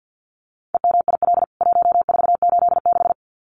Morse code, heard as a keyed tone, about 48 wpm, 700 Hz. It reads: EMIL 1486
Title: CWT Exchange 50wpm